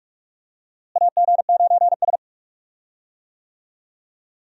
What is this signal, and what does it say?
Morse code, heard as a keyed tone, about 44 wpm, 700 Hz. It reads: AG9S
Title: USA Call Signs - 4 Characters - Repeated 2x - 45wpm